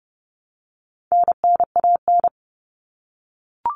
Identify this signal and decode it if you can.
Morse code, heard as a keyed tone, about 30 wpm, 700 Hz. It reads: NNAN
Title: All Cut Numbers - 4-Digit 30wpm